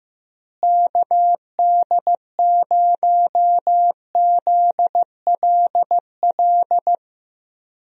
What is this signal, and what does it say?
Morse code, heard as a keyed tone, about 15 wpm, 700 Hz. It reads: KD0ZLL
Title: USA Call Signs - 6 Characters - Repeated 2x - 15wpm